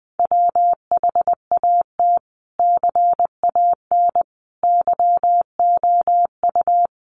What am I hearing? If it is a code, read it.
Morse code, heard as a keyed tone, about 20 wpm, 700 Hz. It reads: WHAT CAN YOU
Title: Sentences from Top 1500 Words 20wpm